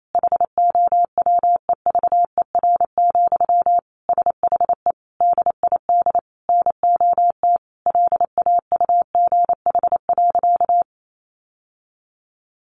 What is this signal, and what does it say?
Morse code, heard as a keyed tone, about 28 wpm, 700 Hz. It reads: HOWEVER, SHE DID NOT LAUGH.
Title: Sentences from Top 1100 Words 28wpm